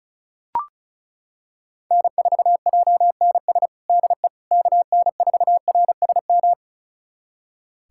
Morse code, heard as a keyed tone, about 36 wpm, 700 Hz. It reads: N4JNS DE KN4RSM
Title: USA Callsign DE Callsign 35wpm